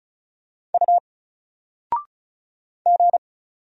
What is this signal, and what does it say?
Morse code, heard as a keyed tone, about 36 wpm, 700 Hz. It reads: U G
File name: Single-Letter-Number-35wpm